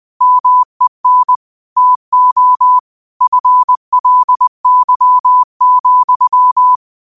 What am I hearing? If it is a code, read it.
Morse code, heard as a keyed tone, about 20 wpm, 1000 Hz. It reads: MEN TO FLY,